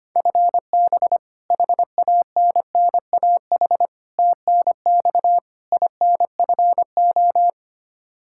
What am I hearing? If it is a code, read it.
Morse code, heard as a keyed tone, about 25 wpm, 700 Hz. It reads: FB HANNAH TNX INFO
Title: Practice QSO - Exchange 3 - More Info - 25wpm